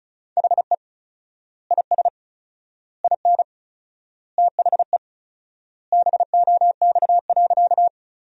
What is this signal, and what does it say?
Morse code, heard as a keyed tone, about 36 wpm, 700 Hz. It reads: HE IS IN THE BOX.
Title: 4x Word-Spacing - Sentences from Top 500 Words - 35wpm